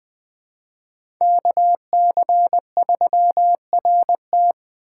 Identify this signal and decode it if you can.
Morse code, heard as a keyed tone, about 20 wpm, 700 Hz. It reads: KC3RT
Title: 2023 SOTA Chaser Call Signs - 20wpm